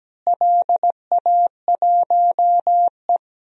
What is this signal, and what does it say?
Morse code, heard as a keyed tone, about 17 wpm, 700 Hz. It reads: LA1E
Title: DXCC-100 Call Signs - 4 Characters - Repeated 3x - 17wpm